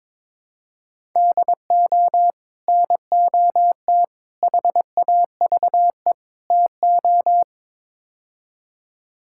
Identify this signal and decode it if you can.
Morse code, heard as a keyed tone, about 22 wpm, 700 Hz. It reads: DO NOT HAVE TO
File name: practice / 22wpm / Build A Phrase - 5-Word Phrases - 22wpm